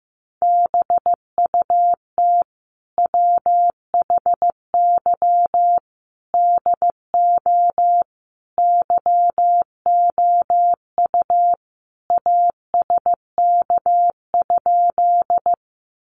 Morse code, heard as a keyed tone, about 15 wpm, 700 Hz. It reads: BUT WHY DO YOU ASK?
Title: Sentences from Top 200 Words 15wpm